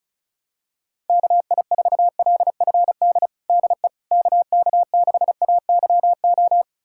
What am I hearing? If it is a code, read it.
Morse code, heard as a keyed tone, about 34 wpm, 700 Hz. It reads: KI4LFD DE KK6AYO
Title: USA Callsign DE Callsign 35wpm